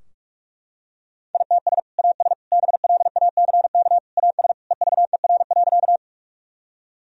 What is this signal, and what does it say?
Morse code, heard as a keyed tone, about 41 wpm, 700 Hz. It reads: ITS AS BLACK AS EVER.
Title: Sentences from Top 1000 Words - Review Part 2 - 45wpm